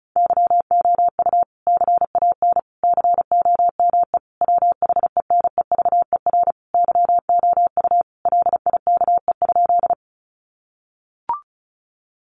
Morse code, heard as a keyed tone, about 34 wpm, 700 Hz. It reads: YOU CAN COME WHENEVER YOU LIKE?
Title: Sentences from Top 1200 Words 35wpm